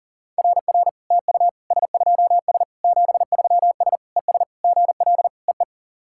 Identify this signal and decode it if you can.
Morse code, heard as a keyed tone, about 40 wpm, 700 Hz. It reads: RR TU S2S 73S ES GL EE
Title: SOTA QSO Practice 40wpm